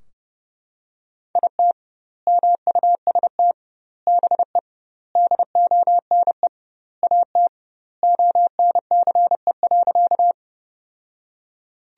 Morse code, heard as a keyed tone, about 30 wpm, 700 Hz. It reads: IT MUST BE DONE AT ONCE.
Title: 2x Word-Spacing - Sentences from Top 500 Words - 30wpm